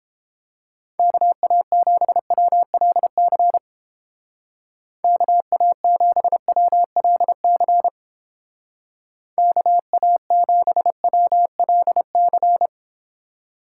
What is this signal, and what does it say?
Morse code, heard as a keyed tone, about 31 wpm, 700 Hz. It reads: KA7WLC KA7WLC KA7WLC